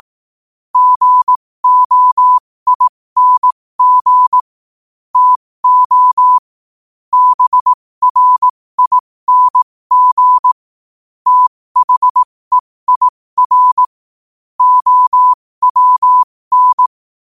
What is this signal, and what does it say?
Morse code, heard as a keyed tone, about 18 wpm, spaced as for 15 wpm, 1000 Hz. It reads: GOING TO BRING THEIR OWN
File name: qotdfiles / 15wpm